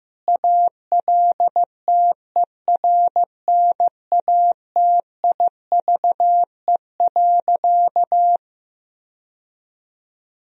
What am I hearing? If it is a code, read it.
Morse code, heard as a keyed tone, about 15 wpm, 700 Hz. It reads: ALTERNATIVE.
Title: Sentences from Top 1000 Words - Review Part 4 - 15wpm